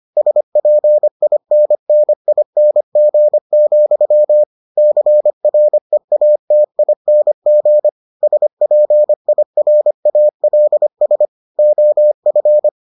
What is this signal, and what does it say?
Morse code, heard as a keyed tone, about 25 wpm, 600 Hz. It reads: SPINNING, CREATING SPIRALS OF